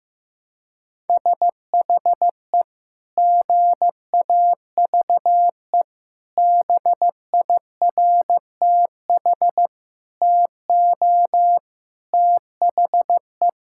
Morse code, heard as a keyed tone, about 15 wpm, 700 Hz. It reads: SHE GAVE BIRTH TO THE
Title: Sentences from Top 1000 Words - Review Part 5 - 15wpm